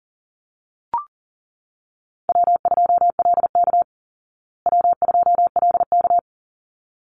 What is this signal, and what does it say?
Morse code, heard as a keyed tone, about 40 wpm, 700 Hz. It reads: W2LK W2LK
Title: Easy Call Signs Repeated 2x - 40wpm